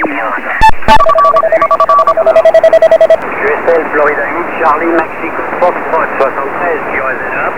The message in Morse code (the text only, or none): none